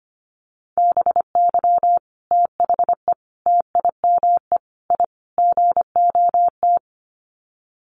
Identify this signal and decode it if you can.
Morse code, heard as a keyed tone, about 25 wpm, 700 Hz. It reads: BY THE TIME I GOT
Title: Sets of 5 Words Encore2 25wpm